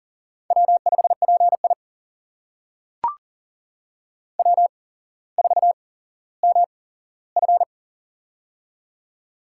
Morse code, heard as a keyed tone, about 39 wpm, spaced as for 17 wpm, 700 Hz. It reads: W5PI W4MF
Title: USA Call Signs - 8x Character Spacing - 40wpm